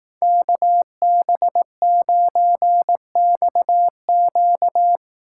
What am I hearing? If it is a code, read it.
Morse code, heard as a keyed tone, about 18 wpm, 700 Hz. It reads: KB9XQ
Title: Speed Racing - USA 2x2 Call Signs 15wpm